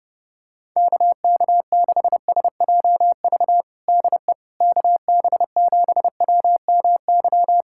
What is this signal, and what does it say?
Morse code, heard as a keyed tone, about 31 wpm, 700 Hz. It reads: KK6SJV DE KB7WMY
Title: USA Callsign DE Callsign - 6 Characters - 30wpm